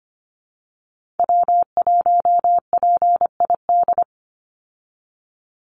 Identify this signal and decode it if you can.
Morse code, heard as a keyed tone, about 25 wpm, 700 Hz. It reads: W1PID
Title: Call Signs Repeated 3x - 25wpm